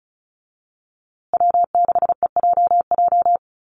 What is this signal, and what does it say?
Morse code, heard as a keyed tone, about 34 wpm, 700 Hz. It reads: W6EJJ
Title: USA Call Signs Repeated 3x 35wpm